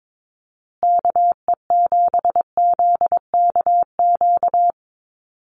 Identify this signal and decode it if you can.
Morse code, heard as a keyed tone, about 22 wpm, 700 Hz. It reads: KE7ZKQ
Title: USA Call Signs Repeated 2x 22wpm